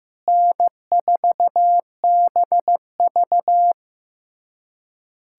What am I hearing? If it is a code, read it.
Morse code, heard as a keyed tone, about 15 wpm, 700 Hz. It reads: N4BV
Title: USA Callsign DE Callsign - 4 Characters - 15wpm